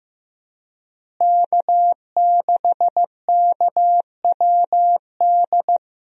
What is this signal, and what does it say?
Morse code, heard as a keyed tone, about 15 wpm, 700 Hz. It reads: K6KWD
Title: USA Call Signs - 5 Characters - Repeated 3x - 15wpm